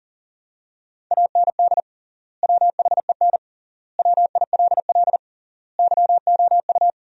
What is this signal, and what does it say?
Morse code, heard as a keyed tone, about 37 wpm, 700 Hz. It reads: AND WHEN WILL YOU
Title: 3x Word-Spacing - Sentences from Top 100 Words - 40wpm